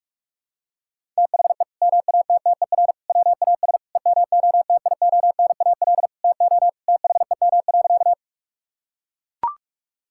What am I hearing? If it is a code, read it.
Morse code, heard as a keyed tone, about 41 wpm, 700 Hz. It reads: THE MATTER WAS EMOTIONAL TO THEM.
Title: Sentences from Top 1000 Words - Review Part 5 - 45wpm